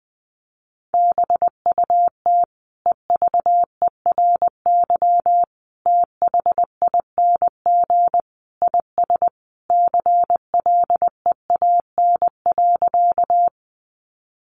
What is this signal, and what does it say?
Morse code, heard as a keyed tone, about 20 wpm, 700 Hz. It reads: BUT EVERY THING IS CLEAN.
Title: Sentences from Top 1100 Words 20wpm